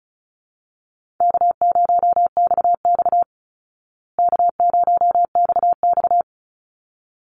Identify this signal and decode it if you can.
Morse code, heard as a keyed tone, about 34 wpm, 700 Hz. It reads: K0XX K0XX
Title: Easy Call Signs Repeated 2x - 35wpm